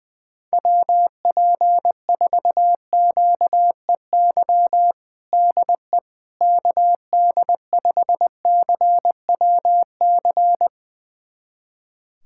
Morse code, heard as a keyed tone, about 20 wpm, 700 Hz. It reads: WP4QEY DE KD5CWC
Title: USA Callsign DE Callsign 20wpm